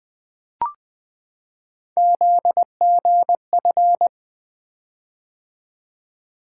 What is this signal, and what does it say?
Morse code, heard as a keyed tone, about 20 wpm, 700 Hz. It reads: ZGF
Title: Three Letter-Number Combinations - 20wpm